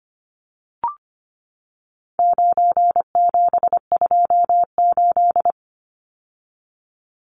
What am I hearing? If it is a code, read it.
Morse code, heard as a keyed tone, about 25 wpm, 700 Hz. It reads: 9728
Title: Numbers - 4-Digit 25wpm